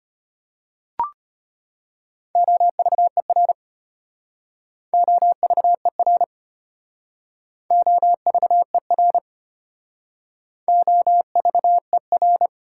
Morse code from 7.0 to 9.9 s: OVER

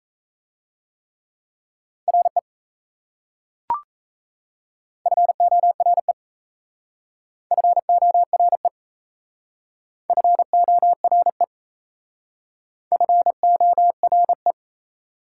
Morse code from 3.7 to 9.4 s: FORE FORE

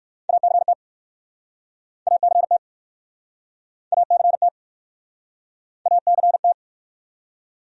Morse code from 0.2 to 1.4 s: ACT

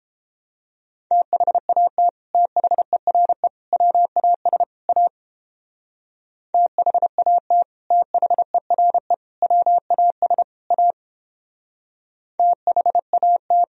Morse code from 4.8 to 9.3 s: A THAT THERE

